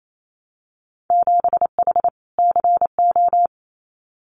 Morse code, heard as a keyed tone, about 28 wpm, 700 Hz. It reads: 7H CO
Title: Winter Field Day Exchanges Encore 28wpm